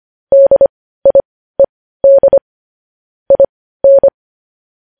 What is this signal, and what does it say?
Morse code, heard as a keyed tone, about 25 wpm, spaced as for 15 wpm, 550 Hz. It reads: DIED IN
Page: news headlines in morse code at 15 wpm